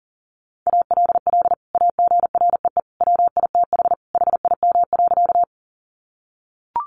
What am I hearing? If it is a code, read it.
Morse code, heard as a keyed tone, about 40 wpm, 700 Hz. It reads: ALL AGREE WITH HIM.
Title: Sentences from Top 1100 Words 40wpm